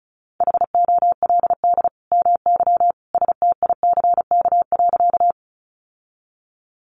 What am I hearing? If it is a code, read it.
Morse code, heard as a keyed tone, about 34 wpm, 700 Hz. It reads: HOLD MY STICK.
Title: Sentences from Top 1200 Words 35wpm